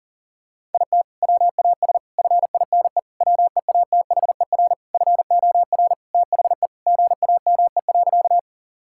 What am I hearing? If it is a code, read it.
Morse code, heard as a keyed tone, about 37 wpm, 690 Hz. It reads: IT WAS FINE WEATHER FOR THE GAME.
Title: Sentences from Top 1000 Words - Review Part 5 - 40wpm